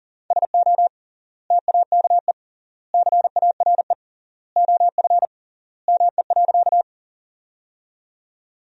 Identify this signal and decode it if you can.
Morse code, heard as a keyed tone, about 40 wpm, 700 Hz. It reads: SO TAKE CARE OF ME.
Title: 3x Word-Spacing - Sentences from Top 300 Words - 40wpm